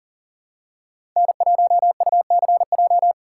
Morse code, heard as a keyed tone, about 40 wpm, 700 Hz. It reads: N1UCJ
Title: USA Call Signs - 5 Characters - 2x Character Spacing - 40wpm